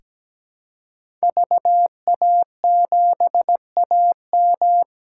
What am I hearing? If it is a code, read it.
Morse code, heard as a keyed tone, about 17 wpm, 700 Hz. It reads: VA7AM
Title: K1USN SST QSO Practice - 17wpm